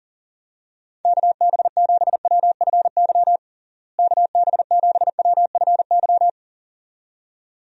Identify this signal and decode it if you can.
Morse code, heard as a keyed tone, about 40 wpm, 700 Hz. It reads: KB7WFY KB7WFY
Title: USA Call Signs - 6 Characters - Repeated 2x - 40wpm